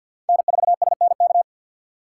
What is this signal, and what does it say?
Morse code, heard as a keyed tone, about 54 wpm, 700 Hz. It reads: N4SNK
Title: USA Call Signs - 5 Characters - 6x Character Spacing - 50wpm